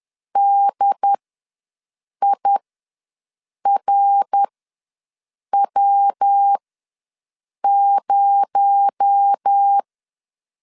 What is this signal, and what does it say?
Morse code, heard as a keyed tone, about 10 wpm, spaced as for 6 wpm, 800 Hz. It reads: DIRW0
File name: LEZIONI / 6 PKX0 / 6 PKX0 ws 10wpm 4220